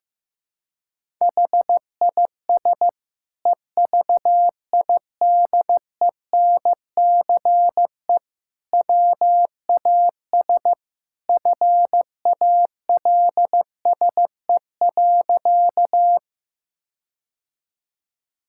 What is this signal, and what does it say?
Morse code, heard as a keyed tone, about 15 wpm, 700 Hz. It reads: HIS EVIDENCE WAS FALSE.
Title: Sentences from Top 1600 Words 15wpm